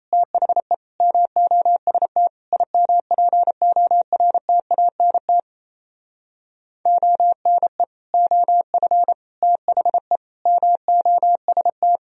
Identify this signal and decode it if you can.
Morse code, heard as a keyed tone, about 31 wpm, 700 Hz. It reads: THE MOST IMPORTANT ONE OF THE MOST
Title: Speed-Racing - Sets of 5 Words 28wpm